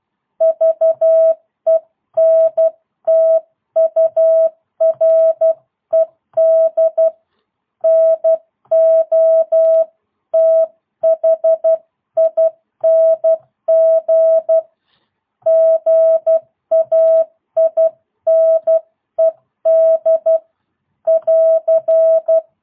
Morse code, heard as a keyed tone, about 12 wpm, 650 Hz. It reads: VENTURED NOTHING GAINED <AR>
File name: jh8cbh / 2022cw